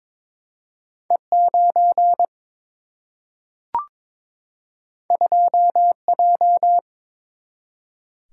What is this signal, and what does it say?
Morse code, heard as a keyed tone, about 22 wpm, 700 Hz. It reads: E 9 2 J
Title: Single-Letter Single-Number 22wpm